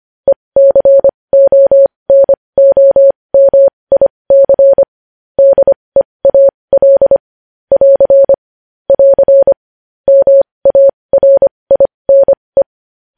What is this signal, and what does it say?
Morse code, heard as a keyed tone, about 25 wpm, spaced as for 20 wpm, 550 Hz. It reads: ECONOMIC DEAL <AR> <AR> MARINE
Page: news headlines in morse code at 20 wpm